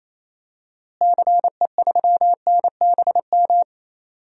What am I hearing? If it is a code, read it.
Morse code, heard as a keyed tone, about 28 wpm, 700 Hz. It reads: CE3NBM
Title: DXCC-100 Call Signs - 6 Characters - Repeated 3x - 28wpm